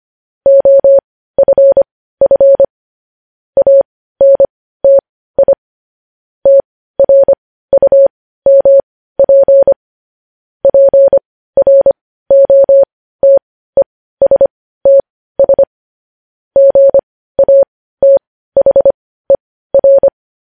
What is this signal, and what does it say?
Morse code, heard as a keyed tone, about 25 wpm, spaced as for 15 wpm, 550 Hz. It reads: OFF ANTI TRUMP PROTESTS GATHER
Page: news-headlines-in-morse-code-at-15-wpm